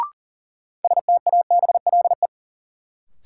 Morse code, heard as a keyed tone, about 40 wpm, 700 Hz. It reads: STABLE
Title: Sentences from Top 1500 Words - New Word Drill Alt - 40wpm